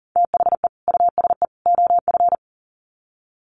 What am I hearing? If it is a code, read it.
Morse code, heard as a keyed tone, about 40 wpm, 700 Hz. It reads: THE USE OF
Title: Sets of 3 Words 40wpm